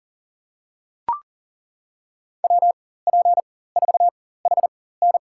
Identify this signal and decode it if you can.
Morse code, heard as a keyed tone, about 40 wpm, spaced as for 19 wpm, 700 Hz. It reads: WP4HN
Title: USA Call Signs - 5 Characters - 4x Character Spacing - 40wpm